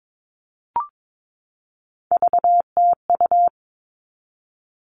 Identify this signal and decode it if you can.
Morse code, heard as a keyed tone, about 22 wpm, 700 Hz. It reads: VTU
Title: All Cut Numbers - 3-Digit 22wpm